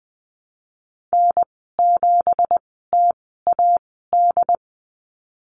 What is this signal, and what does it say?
Morse code, heard as a keyed tone, about 19 wpm, spaced as for 14 wpm, 700 Hz. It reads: N7TAD
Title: 179 Lesson 29g - Slash - Call Signs 20wpm